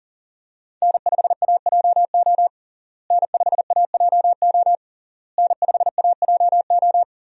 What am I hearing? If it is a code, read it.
Morse code, heard as a keyed tone, about 40 wpm, 700 Hz. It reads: N5AJO N5AJO N5AJO
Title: USA Call Signs - 5 Characters - Repeated 3x - 40wpm